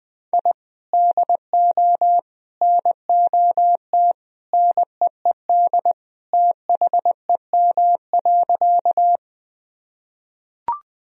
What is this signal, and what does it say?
Morse code, heard as a keyed tone, about 21 wpm, 700 Hz. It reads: I DO NOT NEED THEM.
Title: Sentences from Top 1000 Words - Review Part 1 - 20wpm